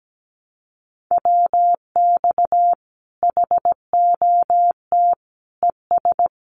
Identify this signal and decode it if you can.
Morse code, heard as a keyed tone, about 17 wpm, 700 Hz. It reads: WX HOT ES
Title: Common QSO Phrases 17wpm